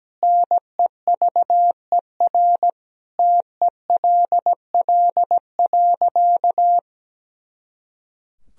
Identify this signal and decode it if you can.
Morse code, heard as a keyed tone, about 17 wpm, 700 Hz. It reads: NEVER TELL.
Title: Sentences from Top 300 Words and Abbreviations 17wpm